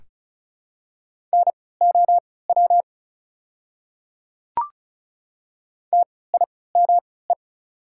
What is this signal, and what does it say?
Morse code, heard as a keyed tone, about 36 wpm, spaced as for 20 wpm, 690 Hz. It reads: NOW TIME
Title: 3x Character Spacing - Top 100 Words - 35wpm